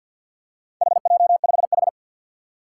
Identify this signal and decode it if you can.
Morse code, heard as a keyed tone, about 47 wpm, 700 Hz. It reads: HJ5H
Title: Four Challenging Letter-Number Combinations 50wpm